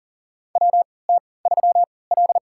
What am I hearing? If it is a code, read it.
Morse code, heard as a keyed tone, about 40 wpm, spaced as for 23 wpm, 700 Hz. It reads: WT3L
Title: USA Call Signs - 4 Characters - 3x Character Spacing - 40wpm